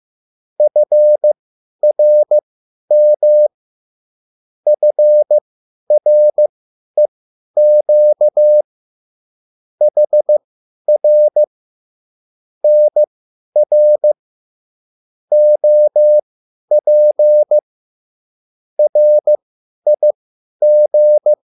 Morse code, heard as a keyed tone, about 15 wpm, spaced as for 10 wpm, 600 Hz. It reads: FRM FREQ HR NR OP RIG